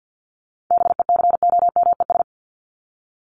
Morse code, heard as a keyed tone, about 50 wpm, 700 Hz. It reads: BECOMES
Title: ES:S Suffix - Top 100 Words with Base Words 50wpm